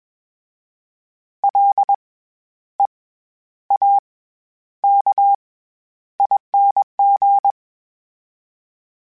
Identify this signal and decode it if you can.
Morse code, heard as a keyed tone, about 21 wpm, spaced as for 11 wpm, 800 Hz. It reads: LEAKING
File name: Special INGs-mp3s / Special INGs-mp3s / SlowToFaster ing @20